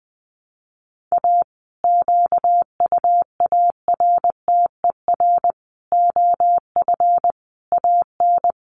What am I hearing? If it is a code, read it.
Morse code, heard as a keyed tone, about 20 wpm, 700 Hz. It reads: A QUARTER OF AN
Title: Sentences from Top 1300 Words 20wpm